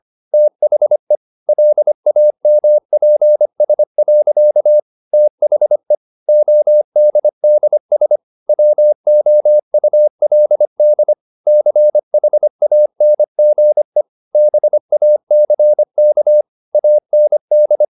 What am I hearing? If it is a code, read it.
Morse code, heard as a keyed tone, about 25 wpm, 600 Hz. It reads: THE LAMPS. THE ODDS WOULD CHANGE BACK AND